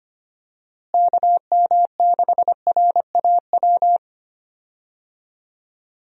Morse code, heard as a keyed tone, about 25 wpm, 700 Hz. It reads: KM6RAW
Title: USA Call Signs - 6 Characters - Repeated 3x - 25wpm